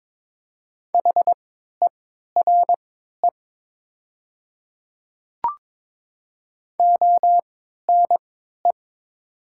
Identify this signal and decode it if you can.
Morse code, heard as a keyed tone, about 22 wpm, spaced as for 12 wpm, 700 Hz. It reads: HERE ONE
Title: 3x Character Spacing - Top 300 Words - 22wpm